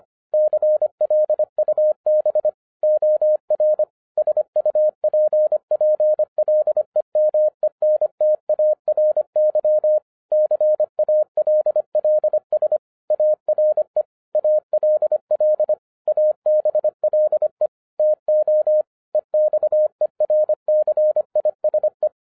Morse code, heard as a keyed tone, about 25 wpm, 610 Hz. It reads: CLUB OR SUPPLEMENTARY CALLS ARE ALL ABLE TO EXERCISE
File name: 2012-01-mp3 / n14-FUN WITH NUMBERS-25wpm